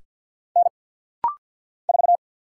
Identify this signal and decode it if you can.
Morse code, heard as a keyed tone, about 50 wpm, 700 Hz. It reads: N 4
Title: Single Letter-Number - Rapid-Fire 50wpm